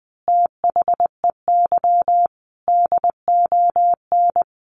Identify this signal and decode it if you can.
Morse code, heard as a keyed tone, about 20 wpm, 700 Hz. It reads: THEY DON
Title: Easy Phrases 20wpm